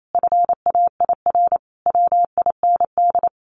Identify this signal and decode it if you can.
Morse code, heard as a keyed tone, about 28 wpm, 700 Hz. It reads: FAIR WIND
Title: Sentences from Top 1400 Words 28wpm